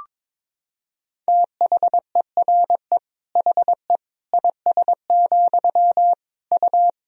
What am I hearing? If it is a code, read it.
Morse code, heard as a keyed tone, about 22 wpm, 700 Hz. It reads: THERE HE IS, U
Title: Sentences from Top 100 Words and Abbreviations 22wpm